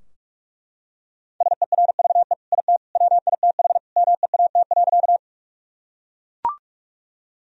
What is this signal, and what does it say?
Morse code, heard as a keyed tone, about 48 wpm, 700 Hz. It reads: SERVE IT WITH MEAT.
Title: Sentences from Top 1000 Words - Review Part 3 - 45wpm